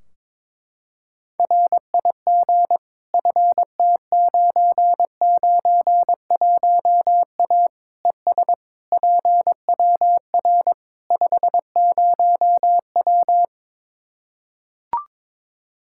Morse code, heard as a keyed tone, about 22 wpm, 700 Hz. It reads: RIG FT991A ES PWR 50W
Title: K1USN SST QSO Practice 22wpm